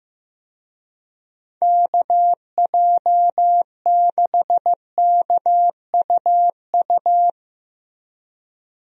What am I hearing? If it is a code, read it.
Morse code, heard as a keyed tone, about 14 wpm, 700 Hz. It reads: KJ6KUU
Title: USA Call Signs - 6 Characters - Repeated 3x - 15wpm